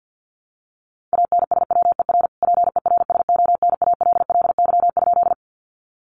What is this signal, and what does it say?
Morse code, heard as a keyed tone, about 50 wpm, 700 Hz. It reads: ANSWER PERSONALLY?
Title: Sentences from Top 1400 Words 50wpm